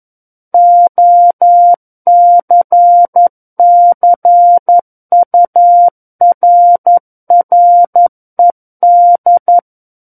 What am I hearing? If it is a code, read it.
Morse code, heard as a keyed tone, about 11 wpm, 700 Hz. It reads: OCCURRED
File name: Book One / 02-Chapter 2 - 11 WPM